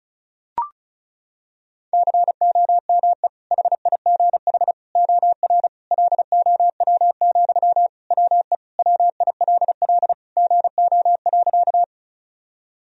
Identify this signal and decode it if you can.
Morse code, heard as a keyed tone, about 36 wpm, 700 Hz. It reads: COME HIGH OR LOW, WE WILL GO.
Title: Sentences from Top 1000 Words - Review Part 1 - 35wpm